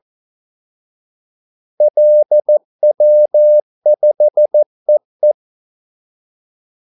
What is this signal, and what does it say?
Morse code, heard as a keyed tone, about 14 wpm, 600 Hz. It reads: LW5EE
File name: Callsigns-14wpm